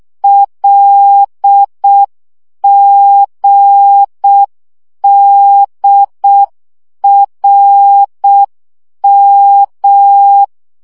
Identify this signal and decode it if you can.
Morse code, heard as a keyed tone, about 6 wpm, 800 Hz. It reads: LGDRM
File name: morze vizsga 2011-2 5